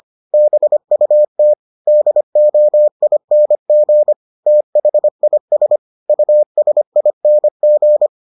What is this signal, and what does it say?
Morse code, heard as a keyed tone, about 25 wpm, 600 Hz. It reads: BUT DOING THIS USING